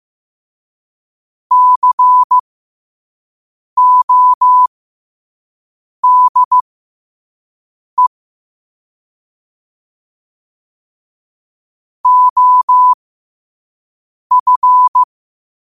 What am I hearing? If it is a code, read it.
Morse code, heard as a keyed tone, about 15 wpm, spaced as for 5 wpm, 1000 Hz. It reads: CODE OF
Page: New quotes every day in morse code at 5 Words per minute.